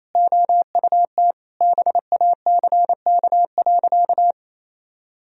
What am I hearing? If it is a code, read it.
Morse code, heard as a keyed tone, about 27 wpm, 700 Hz. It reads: OUT BACK.
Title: Sentences from Top 1000 Words - Review Part 2 - 28wpm